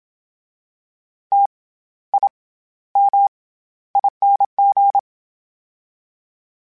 Morse code, heard as a keyed tone, about 27 wpm, 800 Hz. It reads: T I M ING
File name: Special INGs-mp3s / Special INGs-mp3s / SlowToFaster ing @25